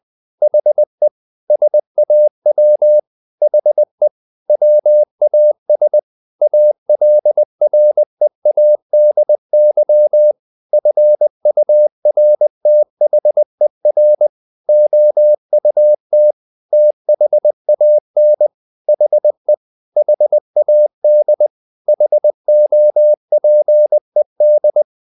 Morse code, heard as a keyed tone, about 20 wpm, 600 Hz. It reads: HE SAW HE WAS ALREADY FURTHER OUT THAN HE HAD HOPED
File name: the old man and the sea / the old man and the sea part 1 20wpm